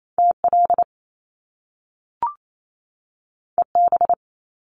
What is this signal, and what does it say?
Morse code, heard as a keyed tone, about 28 wpm, 700 Hz. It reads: TL EB
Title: Top Two Letter Combinations in Words 28wpm